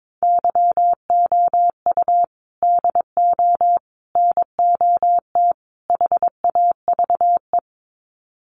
Morse code, heard as a keyed tone, about 22 wpm, 700 Hz. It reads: YOU DO NOT HAVE
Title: Sets of 4 Words 22wpm